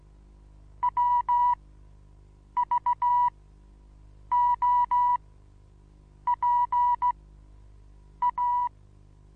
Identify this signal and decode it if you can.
Morse code, heard as a keyed tone, about 16 wpm, spaced as for 7 wpm, 970 Hz. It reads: WVOPA
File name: Onlinelehrgang / morsen / DJ4UF Morsen25